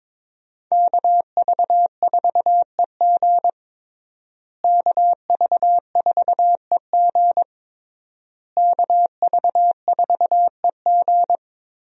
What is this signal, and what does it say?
Morse code, heard as a keyed tone, about 22 wpm, 700 Hz. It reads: KV4EG KV4EG KV4EG
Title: USA Call Signs - 5 Characters - Repeated 3x - 22wpm